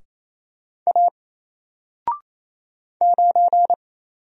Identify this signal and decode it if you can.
Morse code, heard as a keyed tone, about 28 wpm, 700 Hz. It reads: A 9
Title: Single Letter-Number Alt 28wpm